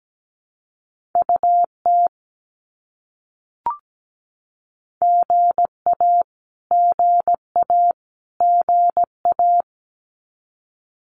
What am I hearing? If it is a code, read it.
Morse code, heard as a keyed tone, about 17 wpm, 700 Hz. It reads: UT GA GA GA
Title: US States Repeated 3x 17wpm